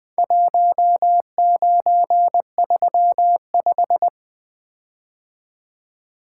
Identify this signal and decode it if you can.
Morse code, heard as a keyed tone, about 20 wpm, 700 Hz. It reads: 1935
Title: Four Letter-Number Combinations - 20wpm